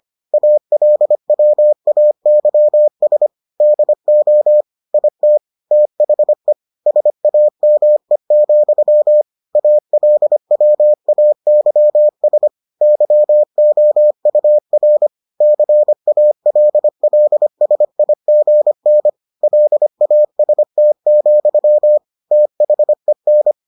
Morse code, heard as a keyed tone, about 25 wpm, 600 Hz. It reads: ALWAYS DO IT THE SAME, ALWAYS YOUR CALLSIGN LAST, THEN